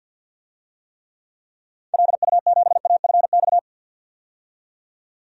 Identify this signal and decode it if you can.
Morse code, heard as a keyed tone, about 51 wpm, 700 Hz. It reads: LU7AFX
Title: DXCC-100 Call Signs - 6 Characters - 50wpm